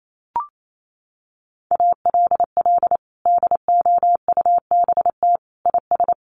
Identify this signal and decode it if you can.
Morse code, heard as a keyed tone, about 28 wpm, 700 Hz. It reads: ALL DOUBT IS
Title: Sentences from Top 1500 Words 28wpm